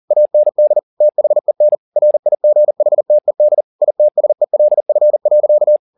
Morse code, heard as a keyed tone, about 40 wpm, 600 Hz. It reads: AND THEN RIGHTED ITSELF.